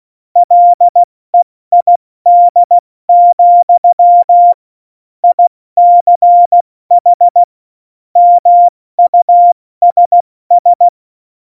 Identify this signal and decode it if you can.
Morse code, heard as a keyed tone, about 16 wpm, 700 Hz. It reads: LEID, ICH MUSS